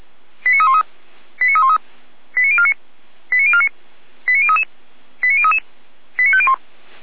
Five-tone selective call (ZVEI), heard as quick runs of five tones, tones of about 70 ms. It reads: 89313 89313 89049 89049 89033 89033 89501
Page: /455/SDR/ZVEI/89305/